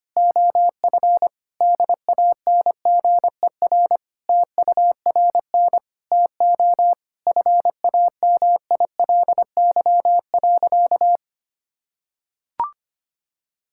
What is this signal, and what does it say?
Morse code, heard as a keyed tone, about 25 wpm, 700 Hz. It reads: OF DANGER TURN TO FAMILY.
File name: practice / 25wpm / Sentences with 3-Word Phrases - 25wpm